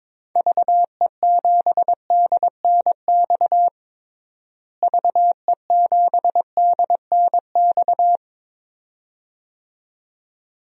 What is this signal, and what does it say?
Morse code, heard as a keyed tone, about 22 wpm, 700 Hz. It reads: VE7DNX VE7DNX
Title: DXCC-100 Call Signs - 6 Characters - Repeated 2x - 22wpm